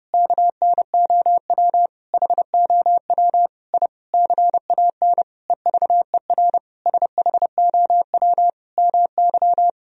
Morse code, heard as a keyed tone, about 30 wpm, 700 Hz. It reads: KNOW HOW I CAN EVER SHOW MY
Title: Sentences from Top 1000 Words - Review Part 2 - 30wpm